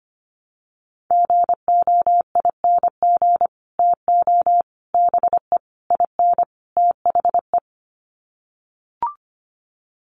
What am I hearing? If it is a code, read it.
Morse code, heard as a keyed tone, about 25 wpm, 700 Hz. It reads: GOING TO BE IN THE
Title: Sets of 5 Words Encore 25wpm